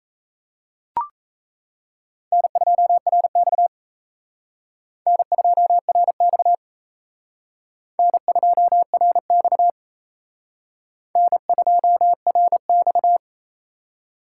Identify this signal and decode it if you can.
Morse code, heard as a keyed tone, about 37 wpm, 700 Hz. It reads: N2RX N2RX N2RX N2RX